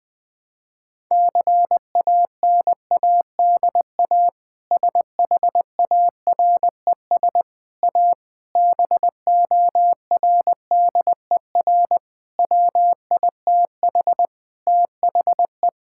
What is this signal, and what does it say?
Morse code, heard as a keyed tone, about 20 wpm, 700 Hz. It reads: CANADA SHARES A BORDER WITH THE
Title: Sentences with 4-Word Phrases - 20wpm